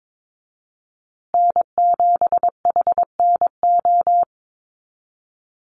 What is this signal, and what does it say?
Morse code, heard as a keyed tone, about 22 wpm, 700 Hz. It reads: N7HNO
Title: USA Call Signs Repeated 2x 22wpm